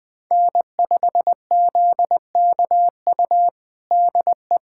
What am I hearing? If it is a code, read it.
Morse code, heard as a keyed tone, about 21 wpm, 700 Hz. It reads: N5ZKU DE
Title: USA Callsign DE Callsign - 5 Characters - 20wpm